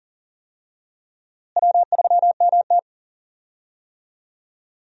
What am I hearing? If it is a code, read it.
Morse code, heard as a keyed tone, about 40 wpm, 700 Hz. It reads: W3MT
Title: USA Call Signs - 4 Characters - Repeated 3x - 40wpm